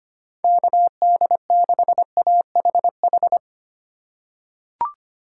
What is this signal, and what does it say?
Morse code, heard as a keyed tone, about 25 wpm, 700 Hz. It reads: KD6AHH
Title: USA Call Signs - 6 Characters - 2x Character Spacing - 25wpm